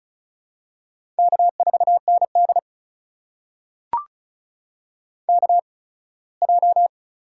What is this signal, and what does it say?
Morse code, heard as a keyed tone, about 36 wpm, spaced as for 21 wpm, 700 Hz. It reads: K4ND KJ
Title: USA Call Signs - 8x Character Spacing - 35wpm